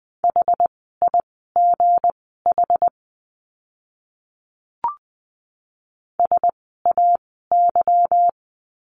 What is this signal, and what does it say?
Morse code, heard as a keyed tone, about 20 wpm, spaced as for 15 wpm, 700 Hz. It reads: HIGH SAY
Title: 192 Lesson 31f - Letter v - Words 20wpm